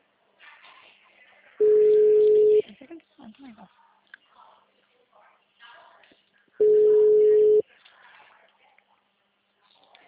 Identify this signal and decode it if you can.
Ringback tone (Europe, 425 Hz), with speech.